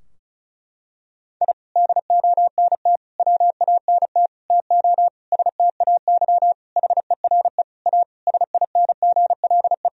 Morse code, heard as a keyed tone, about 34 wpm, 700 Hz. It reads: I DONT WANT TO STAY HERE A SINGLE
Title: Sentences from Top 1000 Words - Review Part 4 - 35wpm